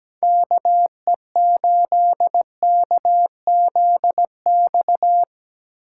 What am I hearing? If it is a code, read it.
Morse code, heard as a keyed tone, about 18 wpm, 700 Hz. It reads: KE8KZX
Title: Speed Racing - USA 2x3 Call Signs 17wpm